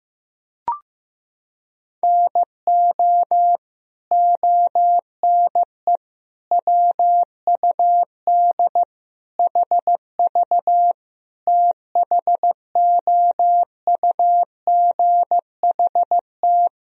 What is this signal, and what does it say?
Morse code, heard as a keyed tone, about 15 wpm, 700 Hz. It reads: NO ONE WUD HV THOUGHT
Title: Sentences from Top 300 Words and Abbreviations 15wpm